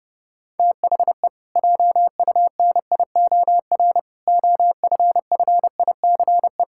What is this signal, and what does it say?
Morse code, heard as a keyed tone, about 30 wpm, 700 Hz. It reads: THE JUNIOR OFFICE
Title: Pangram Challenge v2 - 30wpm